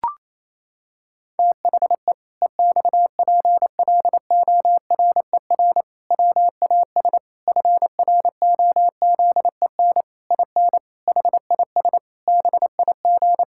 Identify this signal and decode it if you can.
Morse code, heard as a keyed tone, about 28 wpm, 700 Hz. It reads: THE EXPLORER WAS FROZEN IN HIS BIG
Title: Pangram Challenge v2 - 28wpm